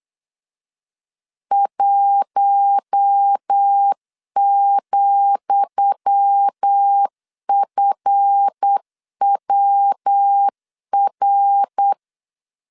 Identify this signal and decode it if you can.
Morse code, heard as a keyed tone, about 8 wpm, 800 Hz. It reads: 1,FWR
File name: LEZIONI / 10 SIGNS / 10D COMA 8wpm 3749